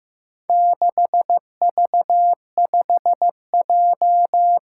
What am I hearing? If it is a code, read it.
Morse code, heard as a keyed tone, about 15 wpm, 700 Hz. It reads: 6V5J
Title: Four Challenging Letter-Number Combinations 15wpm